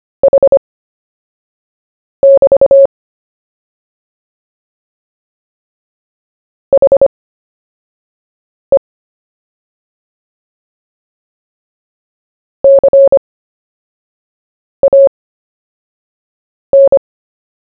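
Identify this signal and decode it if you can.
Morse code, heard as a keyed tone, about 25 wpm, spaced as for 5 wpm, 550 Hz. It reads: H<BT> HE CAN
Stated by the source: CW story-telling practice, Short Story 1 sent at 5 wpm, by VK4JE, Queensland, Australia.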